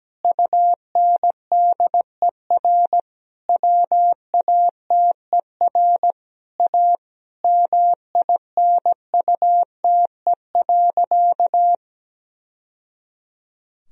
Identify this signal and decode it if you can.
Morse code, heard as a keyed tone, about 17 wpm, 700 Hz. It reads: UNDER WATER A MINUTE.
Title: Sentences from Top 1000 Words - Review Part 1 - 17wpm